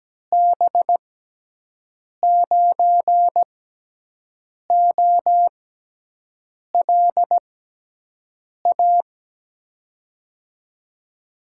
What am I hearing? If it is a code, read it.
Morse code, heard as a keyed tone, about 17 wpm, spaced as for 6 wpm, 700 Hz. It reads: B9OLA
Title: USA Call Signs - 6 Characters - 6x Character Spacing - 17wpm